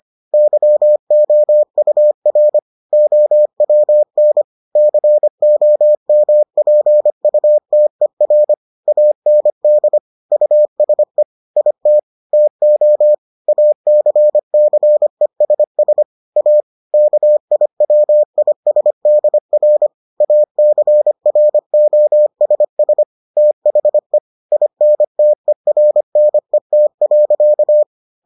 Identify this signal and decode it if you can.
Morse code, heard as a keyed tone, about 25 wpm, 600 Hz. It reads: YOUR OWN COMPUTER AND USE IT TO ACCESS A KIWISDR ACROSS THE INTERNET.